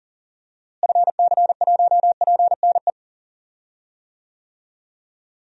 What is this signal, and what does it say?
Morse code, heard as a keyed tone, about 39 wpm, 700 Hz. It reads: FC1PNE
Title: DXCC-100 Call Signs - 6 Characters - Repeated 2x - 40wpm